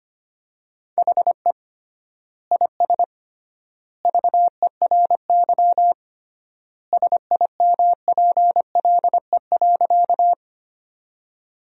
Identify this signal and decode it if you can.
Morse code, heard as a keyed tone, about 25 wpm, 700 Hz. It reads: HE IS VERY SIMPLE.
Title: 3x Word-Spacing - Sentences from Top 400 Words - 25wpm